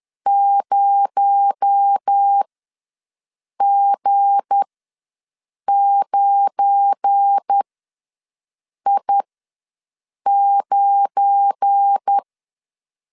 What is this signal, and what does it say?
Morse code, heard as a keyed tone, about 10 wpm, spaced as for 6 wpm, 800 Hz. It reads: 0G9I9
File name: LEZIONI / 9 2867 ws 10wpm 4215